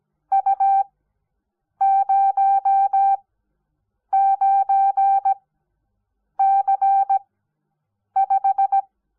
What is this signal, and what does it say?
Morse code, heard as a keyed tone, about 17 wpm, spaced as for 7 wpm, 780 Hz. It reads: U09C5
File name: Onlinelehrgang / morsen / DJ4UF Morsen05